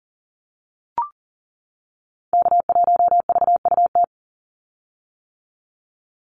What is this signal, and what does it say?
Morse code, heard as a keyed tone, about 40 wpm, 700 Hz. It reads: K1VUT
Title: Call Signs - 40wpm